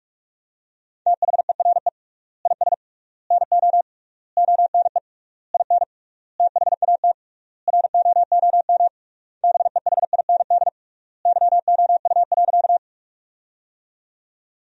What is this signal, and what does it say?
Morse code, heard as a keyed tone, about 42 wpm, 700 Hz. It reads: THERE IS NO ONE IN THAT ROOM BEHIND YOU.
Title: 3x Word-Spacing - Sentences from Top 500 Words - 45wpm